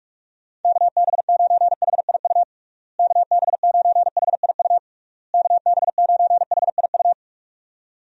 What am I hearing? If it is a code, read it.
Morse code, heard as a keyed tone, about 46 wpm, 700 Hz. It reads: KB9HIU KB9HIU KB9HIU
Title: USA Call Signs - 6 Characters - Repeated 3x - 45wpm